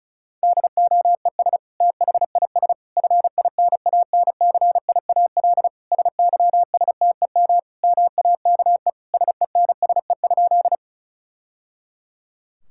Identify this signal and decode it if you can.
Morse code, heard as a keyed tone, about 36 wpm, 700 Hz. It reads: DOES THIS FINANCIAL SYSTEM MAKE SENSE?
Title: Sentences from Top 1000 Words - Review Part 5 - 35wpm